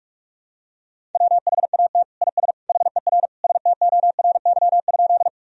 Morse code, heard as a keyed tone, about 46 wpm, 700 Hz. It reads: WHAT IS HER STORY?
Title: Sentences from Top 1000 Words - Review Part 1 - 45wpm